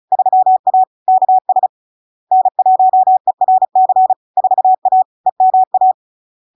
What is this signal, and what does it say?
Morse code, heard as a keyed tone, about 36 wpm, 750 Hz. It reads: 3A KS N1ERC 4A EMA
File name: LARK-FD-2022-Log-in-CW-35wpm-750-Hz